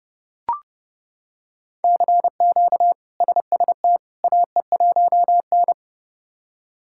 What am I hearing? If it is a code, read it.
Morse code, heard as a keyed tone, about 31 wpm, 700 Hz. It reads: CQ SST AE1N
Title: K1USN SST QSO Practice - 30wpm